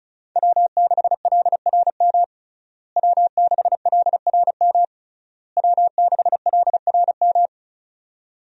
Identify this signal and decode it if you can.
Morse code, heard as a keyed tone, about 36 wpm, 700 Hz. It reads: W6LRM W6LRM W6LRM
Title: USA Call Signs - 5 Characters - Repeated 3x - 35wpm